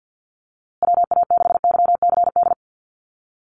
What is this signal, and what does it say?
Morse code, heard as a keyed tone, about 50 wpm, 700 Hz. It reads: WA6YCD
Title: USA Call Signs Repeated 2x 50wpm